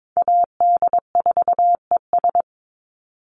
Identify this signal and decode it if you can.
Morse code, heard as a keyed tone, about 22 wpm, 700 Hz. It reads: AD4ES
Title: Call Signs Repeated 3x - 22wpm